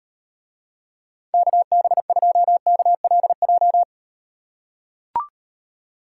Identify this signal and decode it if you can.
Morse code, heard as a keyed tone, about 38 wpm, 700 Hz. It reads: KB2KLJ